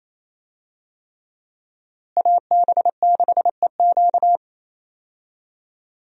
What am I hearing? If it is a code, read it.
Morse code, heard as a keyed tone, about 28 wpm, 700 Hz. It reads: AB6EQ
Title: USA Call Signs - 5 Characters - Repeated 3x - 28wpm